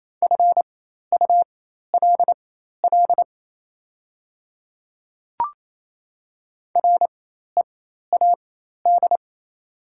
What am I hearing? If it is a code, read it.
Morse code, heard as a keyed tone, about 28 wpm, spaced as for 13 wpm, 700 Hz. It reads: FULL READ
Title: 4x Character Spacing - Top 500 Words - 28wpm